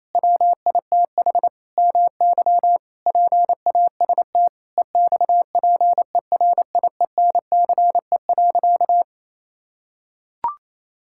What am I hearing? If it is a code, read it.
Morse code, heard as a keyed tone, about 28 wpm, 700 Hz. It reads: WITH MY PAST EXPERIENCE.
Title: Sentences from Top 1000 Words - Review Part 6 - 28wpm